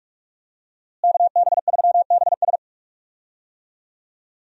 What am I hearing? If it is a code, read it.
Morse code, heard as a keyed tone, about 46 wpm, 700 Hz. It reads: KB3BS
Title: USA Call Signs - 5 Characters - Repeated 3x - 45wpm